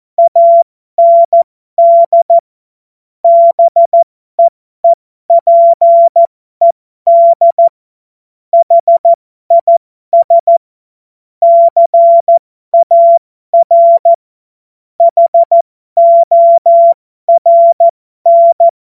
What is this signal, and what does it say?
Morse code, heard as a keyed tone, about 14 wpm, spaced as for 12 wpm, 680 Hz. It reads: AND BEEPED HIS CAR HORN
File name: Morsekurs / Witze